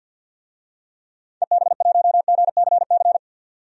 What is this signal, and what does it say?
Morse code, heard as a keyed tone, about 46 wpm, 700 Hz. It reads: EB1GCC
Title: DXCC-100 Call Signs - 6 Characters - 50wpm